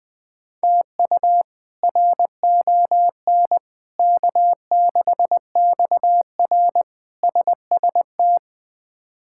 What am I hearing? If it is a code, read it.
Morse code, heard as a keyed tone, about 20 wpm, 700 Hz. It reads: TU RON K6XR SST
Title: K1USN SST QSO Practice - 20wpm